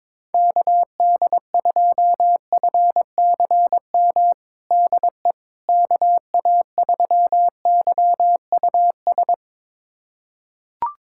Morse code, heard as a keyed tone, about 21 wpm, 700 Hz. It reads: KD2FCM DE KA3YUS
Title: USA Callsign DE Callsign - 6 Characters - 22wpm